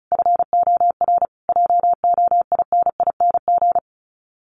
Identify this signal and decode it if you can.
Morse code, heard as a keyed tone, about 34 wpm, 700 Hz. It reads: FOR JOINING
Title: Sets of 5 Words Encore2 35wpm